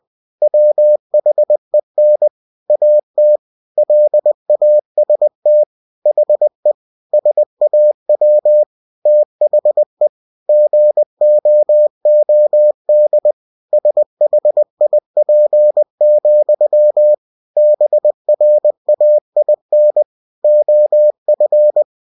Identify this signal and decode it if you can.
Morse code, heard as a keyed tone, about 20 wpm, 600 Hz. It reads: WHEN AT LAST HE SAW THE GOOD SHIP, BRAIN OF
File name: cw-books / winnie-the-pooh / winnie-20wpm-page-0101